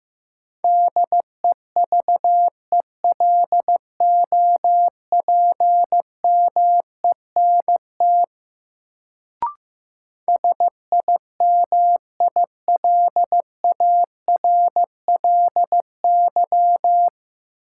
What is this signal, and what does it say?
Morse code, heard as a keyed tone, about 15 wpm, 700 Hz. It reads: DEVELOPMENT SIMILARLY
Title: Sentences from Top 1400 Words - New Word Drill Alt - 15wpm